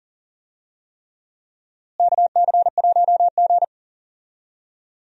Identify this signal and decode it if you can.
Morse code, heard as a keyed tone, about 38 wpm, 700 Hz. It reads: KC1G
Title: USA Call Signs - 4 Characters - Repeated 3x - 40wpm